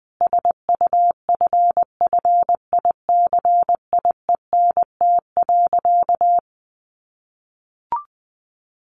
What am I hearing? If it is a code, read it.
Morse code, heard as a keyed tone, about 20 wpm, 700 Hz. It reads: SUFFICIENT.
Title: Sentences from Top 1400 Words 20wpm